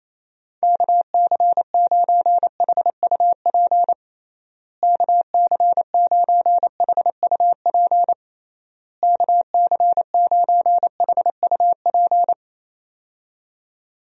Morse code, heard as a keyed tone, about 28 wpm, 700 Hz. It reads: KC9HUP KC9HUP KC9HUP
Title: USA Call Signs - 6 Characters - Repeated 3x - 28wpm